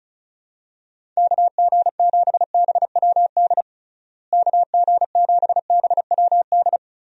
Morse code, heard as a keyed tone, about 36 wpm, 700 Hz. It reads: KG7BWD KG7BWD
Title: USA Call Signs - 6 Characters - Repeated 3x - 35wpm